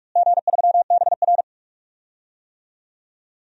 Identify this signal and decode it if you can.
Morse code, heard as a keyed tone, about 45 wpm, 700 Hz. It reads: G3BR
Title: DXCC-100 Call Signs - 4 Characters - Repeated 2x - 45wpm